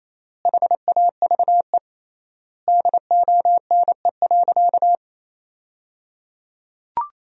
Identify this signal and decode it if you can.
Morse code, heard as a keyed tone, about 28 wpm, 700 Hz. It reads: HAVE DONE.
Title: 3x Word-Spacing - Sentences from Top 500 Words - 28wpm